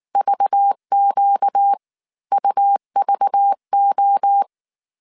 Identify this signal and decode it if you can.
Morse code, heard as a keyed tone, about 19 wpm, spaced as for 16 wpm, 800 Hz. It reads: VQUVO